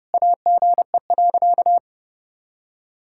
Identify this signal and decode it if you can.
Morse code, heard as a keyed tone, about 30 wpm, 700 Hz. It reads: AGE.
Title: Sentences from Top 1000 Words - Review Part 6 - 30wpm